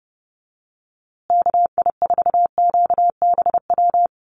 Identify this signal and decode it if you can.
Morse code, heard as a keyed tone, about 30 wpm, 700 Hz. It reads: KI4QBW
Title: USA Call Signs Repeated 2x 30wpm